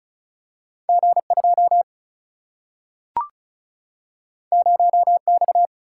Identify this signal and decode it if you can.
Morse code, heard as a keyed tone, about 36 wpm, 700 Hz. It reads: G 2 0 X
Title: Single-Letter Single-Number 35wpm